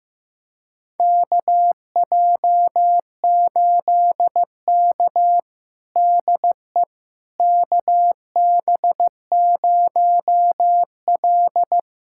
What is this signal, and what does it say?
Morse code, heard as a keyed tone, about 16 wpm, 700 Hz. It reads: KJ8K DE KB0L
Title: USA Callsign DE Callsign - 4 Characters - 15wpm